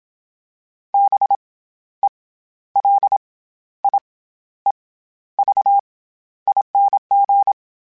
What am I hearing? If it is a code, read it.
Morse code, heard as a keyed tone, about 27 wpm, 800 Hz. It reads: B E L I E V ING
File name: Special INGs-mp3s / Special INGs-mp3s / SlowToFaster ing @25